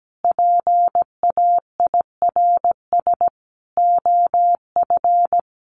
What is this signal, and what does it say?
Morse code, heard as a keyed tone, about 17 wpm, 700 Hz. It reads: PAIRS OF
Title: Additional Easy Phrases 17wpm